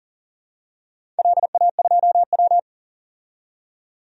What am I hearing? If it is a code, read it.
Morse code, heard as a keyed tone, about 41 wpm, 700 Hz. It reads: LA2W
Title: DXCC-100 Call Signs - 4 Characters - 40wpm